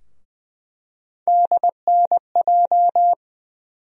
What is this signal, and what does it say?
Morse code, heard as a keyed tone, about 20 wpm, 700 Hz. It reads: DNJ
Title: USA Call Sign Postfixes - 3 Letters 20wpm